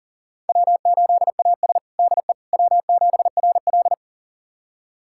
Morse code, heard as a keyed tone, about 40 wpm, 700 Hz. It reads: W8AS DE W7RL
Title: USA Callsign DE Callsign - 4 Characters - 40wpm